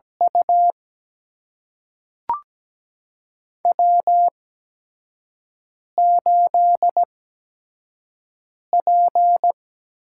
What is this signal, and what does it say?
Morse code, heard as a keyed tone, about 17 wpm, spaced as for 5 wpm, 700 Hz. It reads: U W8P
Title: USA Call Signs - 8x Character Spacing - 17wpm